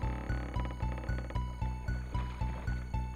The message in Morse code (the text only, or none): none